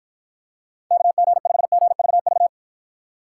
Morse code, heard as a keyed tone, about 53 wpm, 700 Hz. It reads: KG5GVV